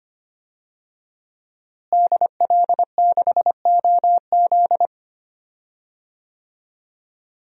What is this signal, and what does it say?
Morse code, heard as a keyed tone, about 25 wpm, 700 Hz. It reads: DL6OZ
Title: DXCC-100 Call Signs - 5 Characters - Repeated 2x - 25wpm